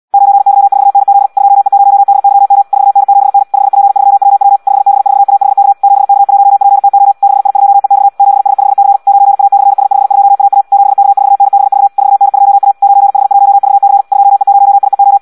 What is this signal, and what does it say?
Morse code, heard as a keyed tone, about 74 wpm, 800 Hz. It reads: 9Y4NZ 2E0DYM 4N7N 5P3WW 5B4AHK 7K1PTO 6I2EX 7S5C 9A/S52AA 7K4NBV 4N2A 9H1VC 2E0IEO
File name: Audio practica CW